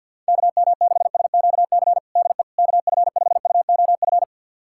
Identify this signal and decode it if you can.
Morse code, heard as a keyed tone, about 48 wpm, 700 Hz. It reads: KM6SQC DE KF5UOF
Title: USA Callsign DE Callsign - 6 Characters - 50wpm